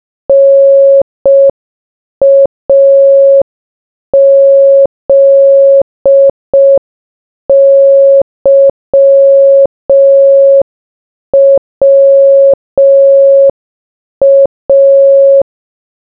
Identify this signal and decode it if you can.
Morse code, heard as a keyed tone, about 5 wpm, 550 Hz. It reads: NAZYWA